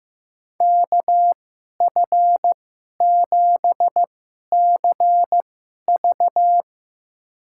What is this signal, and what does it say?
Morse code, heard as a keyed tone, about 16 wpm, spaced as for 11 wpm, 700 Hz. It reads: KF7CV
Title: USA Call Signs - 5 Characters - 2x Character Spacing - 15wpm